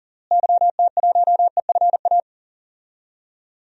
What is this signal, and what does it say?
Morse code, heard as a keyed tone, about 40 wpm, 700 Hz. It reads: YT1EFA
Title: DXCC-100 Call Signs - 6 Characters - Repeated 2x - 40wpm